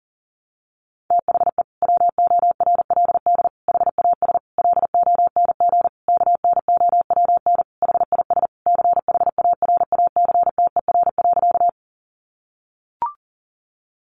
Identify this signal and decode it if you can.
Morse code, heard as a keyed tone, about 40 wpm, 700 Hz. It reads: THE WORLD HAS LONG KNOWN HIS CHARACTER.
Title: Sentences from Top 1200 Words 40wpm